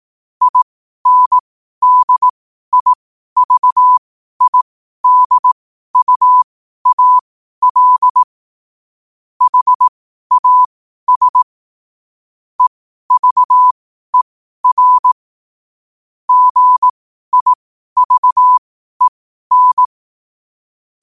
Morse code, heard as a keyed tone, about 18 wpm, spaced as for 12 wpm, 1000 Hz. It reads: INDIVIDUAL HAS EVER GIVEN
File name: qotdfiles / 12wpm